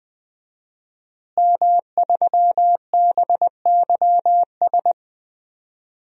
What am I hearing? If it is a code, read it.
Morse code, heard as a keyed tone, about 21 wpm, 700 Hz. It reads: M3BYS
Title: DXCC-100 Call Signs - 5 Characters - Repeated 2x - 20wpm